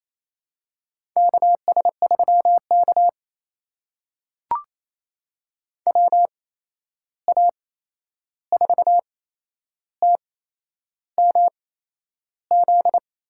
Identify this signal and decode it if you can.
Morse code, heard as a keyed tone, about 27 wpm, spaced as for 10 wpm, 700 Hz. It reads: KS3K WA4TMZ
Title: USA Call Signs - 8x Character Spacing - 28wpm